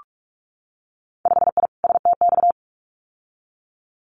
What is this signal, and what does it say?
Morse code, heard as a keyed tone, about 46 wpm, 700 Hz. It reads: 5I STX
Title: Winter Field Day Exchanges Encore 45wpm